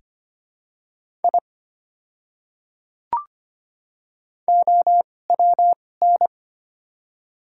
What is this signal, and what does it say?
Morse code, heard as a keyed tone, about 25 wpm, spaced as for 18 wpm, 700 Hz. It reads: I OWN
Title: 2x Character Spacing - Top 500 Words - 25wpm